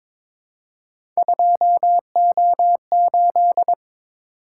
Morse code, heard as a keyed tone, about 22 wpm, 700 Hz. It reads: 2O8
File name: Three Letter-Number Combinations - 22wpm